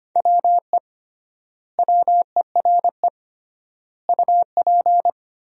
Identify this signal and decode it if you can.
Morse code, heard as a keyed tone, about 25 wpm, 700 Hz. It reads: WE WERE UP
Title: 3x Word-Spacing - Sentences from Top 400 Words - 25wpm